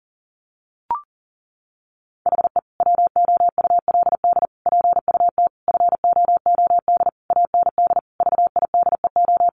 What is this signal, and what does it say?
Morse code, heard as a keyed tone, about 40 wpm, 700 Hz. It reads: HE WOULD PUT FOOD AND VIDEO
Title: Easy Phrases 40wpm